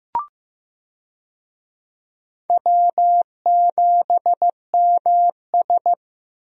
Morse code, heard as a keyed tone, about 16 wpm, 700 Hz. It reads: W7MS
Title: USA Call Signs - 4 Characters - Repeated 3x - 15wpm